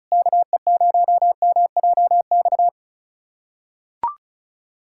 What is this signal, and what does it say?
Morse code, heard as a keyed tone, about 36 wpm, 700 Hz. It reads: KE0MJX
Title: USA Callsign DE Callsign - 6 Characters - 35wpm